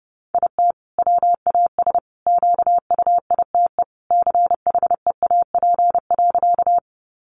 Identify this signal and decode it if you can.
Morse code, heard as a keyed tone, about 30 wpm, 700 Hz. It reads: IT WAS QUITE CHEAP.
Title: Sentences from Top 1200 Words 30wpm